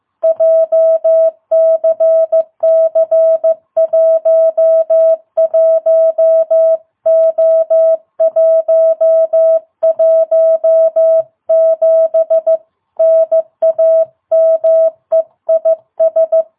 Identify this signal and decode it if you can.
Morse code, heard as a keyed tone, about 15 wpm, 660 Hz. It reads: JCC11O117 NAME IS
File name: jh8cbh / 2022cw